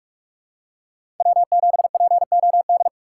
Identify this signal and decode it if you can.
Morse code, heard as a keyed tone, about 46 wpm, 700 Hz. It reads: W7POD
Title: USA Call Signs - 5 Characters - 3x Character Spacing - 45wpm